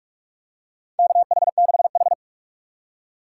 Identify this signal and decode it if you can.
Morse code, heard as a keyed tone, about 46 wpm, 690 Hz. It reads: KH6H
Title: USA Call Signs - 4x Character Spacing - 45wpm